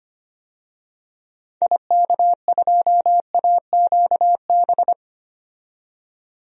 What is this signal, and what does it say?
Morse code, heard as a keyed tone, about 25 wpm, 700 Hz. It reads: IK2AQB
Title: DXCC-100 Call Signs - 6 Characters - 25wpm